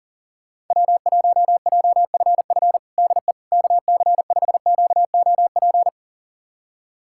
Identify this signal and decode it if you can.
Morse code, heard as a keyed tone, about 38 wpm, 700 Hz. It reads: W1JFF DE KC5QOP
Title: USA Callsign DE Callsign 40wpm